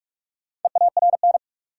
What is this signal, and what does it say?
Morse code, heard as a keyed tone, about 46 wpm, 700 Hz. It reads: EARN
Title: Sentences from Top 1500 Words - New Word Drill - 45wpm